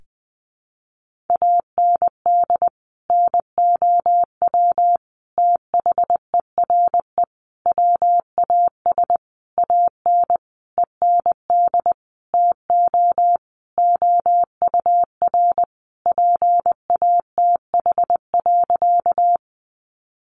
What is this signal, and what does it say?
Morse code, heard as a keyed tone, about 20 wpm, 700 Hz. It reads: AND NOW THERE WAS AN END TO OUR PATH.
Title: Sentences from Top 1400 Words 20wpm